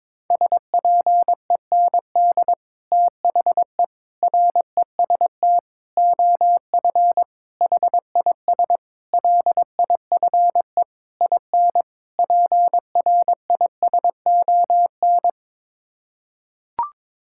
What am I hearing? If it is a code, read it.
Morse code, heard as a keyed tone, about 22 wpm, 700 Hz. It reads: SPEND THE REST OF HIS LIFE IN PRISON
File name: practice / 22wpm / Sets of 8 Words - 22wpm